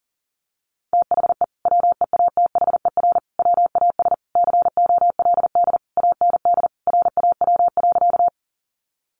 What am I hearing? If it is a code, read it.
Morse code, heard as a keyed tone, about 40 wpm, 700 Hz. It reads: THE WEATHER WAS COLD AND RAW.
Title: Sentences from Top 1500 Words 40wpm